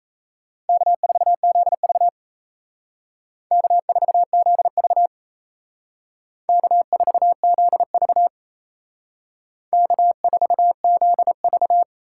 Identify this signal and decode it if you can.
Morse code, heard as a keyed tone, about 32 wpm, 700 Hz. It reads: K4ZV K4ZV K4ZV K4ZV
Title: Speed Racing - USA 1x2 Call Signs 28wpm